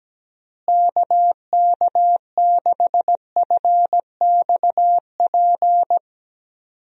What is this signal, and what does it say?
Morse code, heard as a keyed tone, about 17 wpm, 700 Hz. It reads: KK6FXP
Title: USA Call Signs - 6 Characters - Repeated 2x - 17wpm